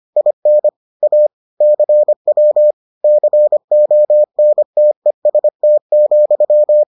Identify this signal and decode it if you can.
Morse code, heard as a keyed tone, about 25 wpm, 600 Hz. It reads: IN A CW CONTEST,